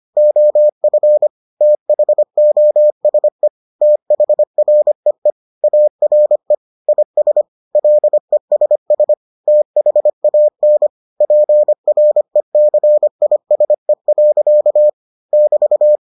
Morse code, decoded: OF THOSE THREE ARE IS LESS THAN PRECISE. <BT>

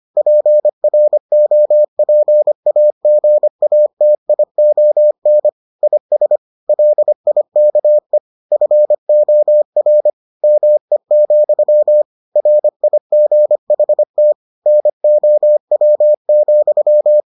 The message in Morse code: PROPAGATION IS LIKE FOR ME, RIGHT NOW,

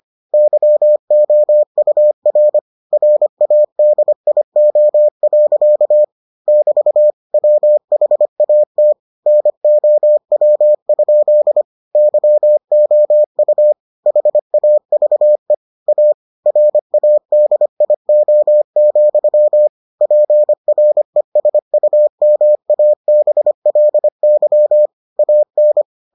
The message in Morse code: YOUR RADIO. <BT> WHAT NOW? YOU HAVE A RADIO, PRESUMABLY AN

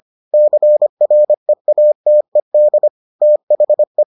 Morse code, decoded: CREATED THE

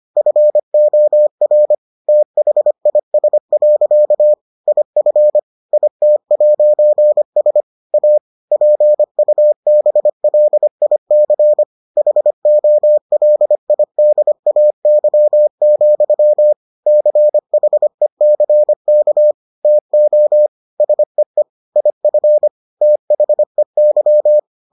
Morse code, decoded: FOR THIS. IF IT'S A PUBLIC HOLIDAY, CHECK TO SEE IF THEY

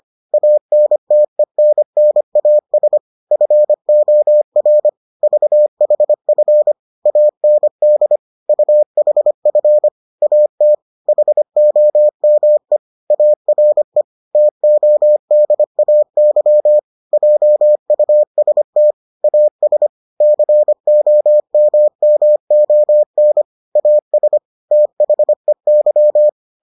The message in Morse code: ANTENNAS FOR VHF AND UHF AT HOME ARE TODAY JUST AS COMMON AS THEY

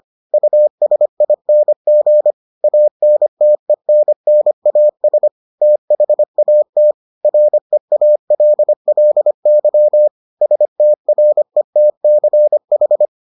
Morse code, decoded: USING ANTENNAS THAT REALLY STRETCH

